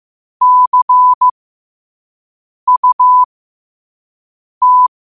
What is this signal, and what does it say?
Morse code, heard as a keyed tone, about 15 wpm, spaced as for 5 wpm, 1000 Hz. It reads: CUT
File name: 5wpm